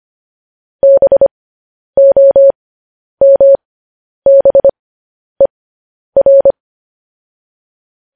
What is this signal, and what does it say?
Morse code, heard as a keyed tone, about 25 wpm, spaced as for 10 wpm, 550 Hz. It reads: BOMBER
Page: news headlines in morse code at 10 wpm